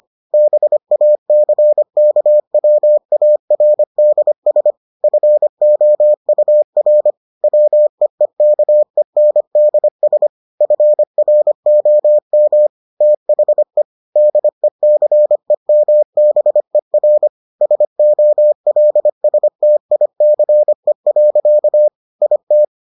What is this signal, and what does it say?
Morse code, heard as a keyed tone, about 25 wpm, 600 Hz. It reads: BACKWARDS FOUR WEEKENDS FROM THE DECEMBER SOLSTICE. IT